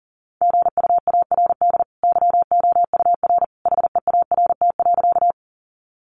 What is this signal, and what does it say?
Morse code, heard as a keyed tone, about 40 wpm, 700 Hz. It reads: GUARD YOUR HEART.
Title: Sentences from Top 1200 Words 40wpm